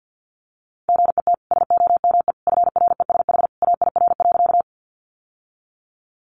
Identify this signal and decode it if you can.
Morse code, heard as a keyed tone, about 50 wpm, 700 Hz. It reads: GET SOME FRESH AIR.
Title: Sentences from Top 1200 Words 50wpm